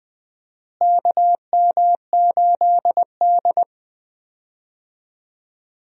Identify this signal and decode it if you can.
Morse code, heard as a keyed tone, about 20 wpm, 700 Hz. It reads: KM8D
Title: USA Call Signs - 4 Characters - Repeated 2x - 20wpm